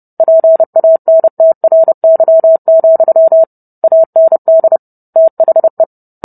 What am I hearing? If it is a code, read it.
Morse code, heard as a keyed tone, about 30 wpm, 650 Hz. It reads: PANTRY, AND THE